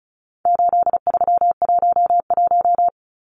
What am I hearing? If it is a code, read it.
Morse code, heard as a keyed tone, about 34 wpm, 700 Hz. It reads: 8311
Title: Numbers - 4-Digit 35wpm